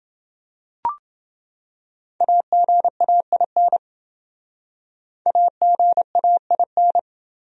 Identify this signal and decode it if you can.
Morse code, heard as a keyed tone, about 28 wpm, 700 Hz. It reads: AGAIN AGAIN